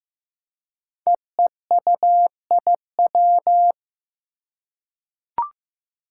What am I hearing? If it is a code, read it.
Morse code, heard as a keyed tone, about 15 wpm, 700 Hz. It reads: EEUIW
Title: Five Challenging Letter-Number Combinations 15wpm